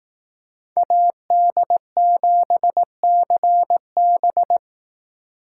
Morse code, heard as a keyed tone, about 18 wpm, 700 Hz. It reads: AD7CB
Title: Speed Racing - USA 2x2 Call Signs 15wpm